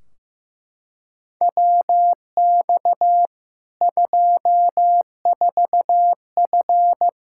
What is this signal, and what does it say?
Morse code, heard as a keyed tone, about 15 wpm, 700 Hz. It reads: WX 24F
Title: K1USN SST QSO Practice 15wpm